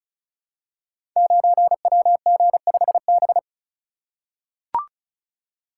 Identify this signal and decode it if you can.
Morse code, heard as a keyed tone, about 36 wpm, 700 Hz. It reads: 9WG5B
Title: Code Groups - 5 Letter:Numbers - 35wpm